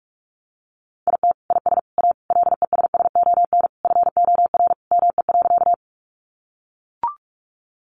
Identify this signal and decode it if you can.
Morse code, heard as a keyed tone, about 44 wpm, 700 Hz. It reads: IT IS A LESSON FOR ME.
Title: Sentences from Top 1200 Words 45wpm